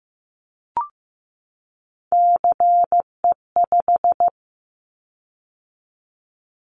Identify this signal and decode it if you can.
Morse code, heard as a keyed tone, about 15 wpm, 700 Hz. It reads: CE5
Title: Callsign Prefixes for Top 100 DXCC - 15wpm